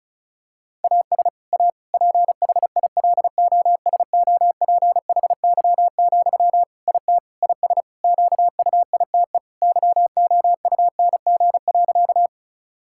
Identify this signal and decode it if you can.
Morse code, heard as a keyed tone, about 36 wpm, 700 Hz. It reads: AS A PHILOSOPHY, IT IS QUITE YOUNG.
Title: Sentences from Top 1000 Words - Review Part 6 - 35wpm